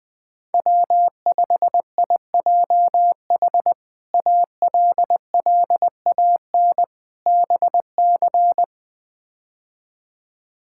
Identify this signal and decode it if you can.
Morse code, heard as a keyed tone, about 20 wpm, 700 Hz. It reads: W5IJH ALLAN BC
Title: K1USN SST QSO Practice - 20wpm